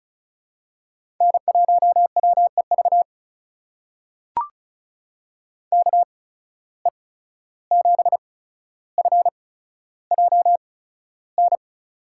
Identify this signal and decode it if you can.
Morse code, heard as a keyed tone, about 36 wpm, spaced as for 14 wpm, 700 Hz. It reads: N1WEV KE7FJN
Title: USA Call Signs - 8x Character Spacing - 35wpm